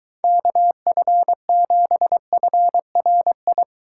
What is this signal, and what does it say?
Morse code, heard as a keyed tone, about 23 wpm, 700 Hz. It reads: KF7FRI